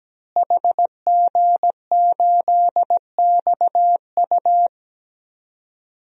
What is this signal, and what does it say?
Morse code, heard as a keyed tone, about 17 wpm, 700 Hz. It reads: HG8XU
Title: DXCC-100 Call Signs - 5 Characters - Repeated 3x - 17wpm